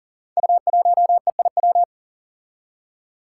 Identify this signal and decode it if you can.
Morse code, heard as a keyed tone, about 38 wpm, 700 Hz. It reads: U1EIW
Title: Five Challenging Letter-Number Combinations 40wpm